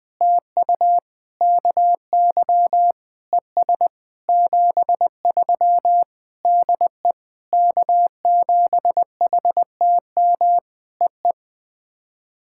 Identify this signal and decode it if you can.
Morse code, heard as a keyed tone, about 19 wpm, 700 Hz. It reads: TU KY ES 73 DE K7HTM EE
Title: POTA QSO Practice - 20wpm